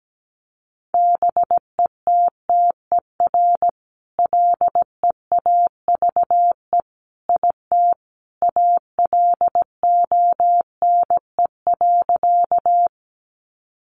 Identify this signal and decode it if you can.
Morse code, heard as a keyed tone, about 17 wpm, 700 Hz. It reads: BETTER LEAVE IT ALONE.
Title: Sentences from Top 1000 Words 17wpm